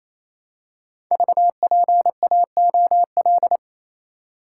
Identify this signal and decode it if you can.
Morse code, heard as a keyed tone, about 27 wpm, 700 Hz. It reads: VPAOL
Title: Code Groups - 5 Letters - 28wpm